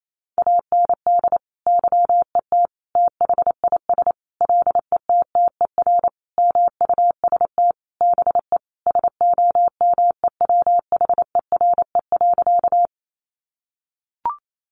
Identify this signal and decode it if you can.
Morse code, heard as a keyed tone, about 28 wpm, 700 Hz. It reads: AND YET THIS LETTER MUST BE SOMEWHERE.
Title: Sentences from Top 1400 Words 28wpm